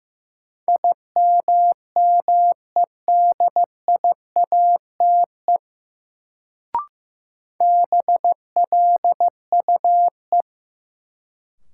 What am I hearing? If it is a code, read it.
Morse code, heard as a keyed tone, about 15 wpm, 700 Hz. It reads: IMMEDIATE BLUE
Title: Sentences from Top 1200 Words - New Word Drill Alt - 15wpm